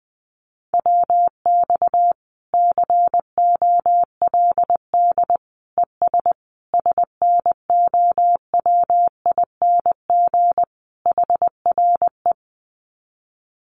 Morse code, heard as a keyed tone, about 19 wpm, 700 Hz. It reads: WX COLD ES SNOWING HRE
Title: Common QSO Phrases 20wpm